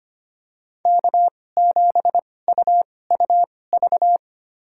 Morse code, heard as a keyed tone, about 25 wpm, spaced as for 18 wpm, 700 Hz. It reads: K7UUV
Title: USA Call Signs - 5 Characters - 2x Character Spacing - 25wpm